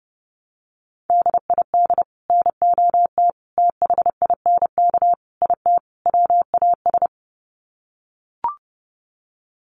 Morse code, heard as a keyed tone, about 30 wpm, 700 Hz. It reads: DID NOT THINK IT WAS
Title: Sets of 5 Words Encore 30wpm